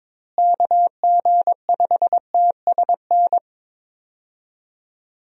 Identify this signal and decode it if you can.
Morse code, heard as a keyed tone, about 22 wpm, 700 Hz. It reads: KG5TSN
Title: USA Call Signs - 6 Characters - Repeated 2x - 22wpm